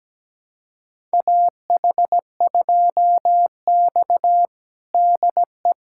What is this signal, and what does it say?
Morse code, heard as a keyed tone, about 17 wpm, 700 Hz. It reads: AH2X DE
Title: USA Callsign DE Callsign - 4 Characters - 17wpm